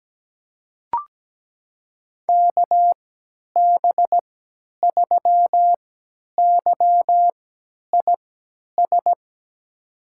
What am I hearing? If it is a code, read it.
Morse code, heard as a keyed tone, about 17 wpm, spaced as for 10 wpm, 700 Hz. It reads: KB3YIS
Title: USA Call Signs - 6 Characters - 3x Character Spacing - 17wpm